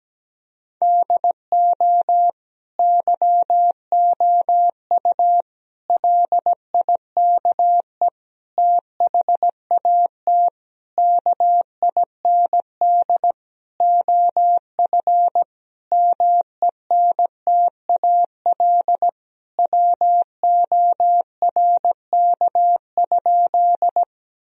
Morse code, heard as a keyed tone, about 17 wpm, 700 Hz. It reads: DO YOU LIKE THAT KIND OF MENTAL WORK?
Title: Sentences from Top 1000 Words - Review Part 4 - 17wpm